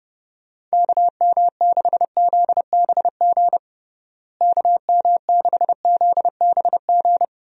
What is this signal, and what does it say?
Morse code, heard as a keyed tone, about 30 wpm, 700 Hz. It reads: KM6ZBG KM6ZBG
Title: USA Call Signs - 6 Characters - Repeated 2x - 30wpm